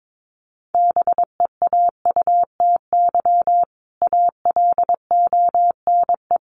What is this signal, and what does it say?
Morse code, heard as a keyed tone, about 22 wpm, 700 Hz. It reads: BEAUTY ALONE
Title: Sentences from Top 1200 Words 22wpm